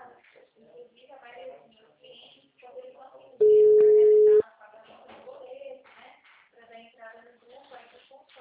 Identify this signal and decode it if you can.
Ringback tone (Europe, 425 Hz), with speech.